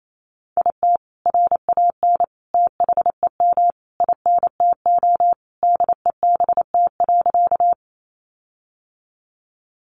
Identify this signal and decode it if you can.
Morse code, heard as a keyed tone, about 28 wpm, 700 Hz. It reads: IT RAN THEM INTO DEBT.
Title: Sentences from Top 1300 Words 28wpm